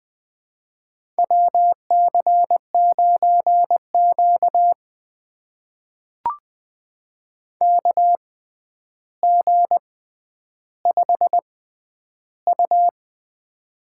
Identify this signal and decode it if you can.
Morse code, heard as a keyed tone, about 20 wpm, spaced as for 10 wpm, 700 Hz. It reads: WC9Q KG5U
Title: USA Call Signs - 6x Character Spacing - 20wpm